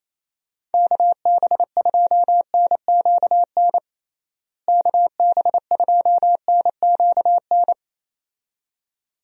Morse code, heard as a keyed tone, about 28 wpm, 700 Hz. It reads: KB2NQN KB2NQN
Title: USA Call Signs - 6 Characters - Repeated 2x - 28wpm